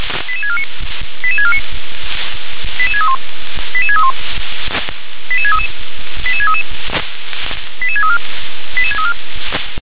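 Five-tone selective call (ZVEI), heard as quick runs of five tones, tones of about 70 ms. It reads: 88649 88649 88631 88631 88633 88633 88634 88634
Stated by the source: ZVEI five-tone selective call received on SDR.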